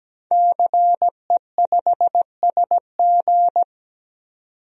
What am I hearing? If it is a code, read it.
Morse code, heard as a keyed tone, about 17 wpm, 700 Hz. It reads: CE5SG
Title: DXCC-100 Call Signs - 5 Characters - 17wpm